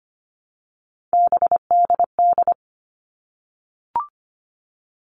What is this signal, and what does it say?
Morse code, heard as a keyed tone, about 25 wpm, 700 Hz. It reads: BDD
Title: All Cut Numbers - 3-Digit 25wpm